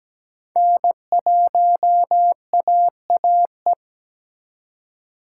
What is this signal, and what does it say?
Morse code, heard as a keyed tone, about 20 wpm, 700 Hz. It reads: N1AAE
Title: Speed Racing - USA 1x3 Call Signs 17wpm